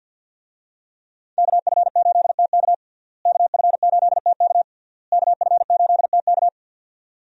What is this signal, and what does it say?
Morse code, heard as a keyed tone, about 52 wpm, 700 Hz. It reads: KF8TK KF8TK KF8TK
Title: USA Call Signs - 5 Characters - Repeated 3x - 50wpm